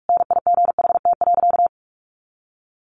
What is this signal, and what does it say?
Morse code, heard as a keyed tone, about 44 wpm, 700 Hz. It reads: NIGHT.
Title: Sentences from Top 1400 Words 45wpm